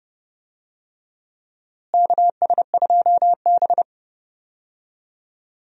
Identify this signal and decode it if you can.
Morse code, heard as a keyed tone, about 30 wpm, 700 Hz. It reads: KS2B
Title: USA Call Signs - 4 Characters - Repeated 3x - 30wpm